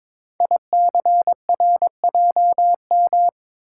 Morse code, heard as a keyed tone, about 22 wpm, 700 Hz. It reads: ICRJM
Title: Code Groups - 5 Letters - 22wpm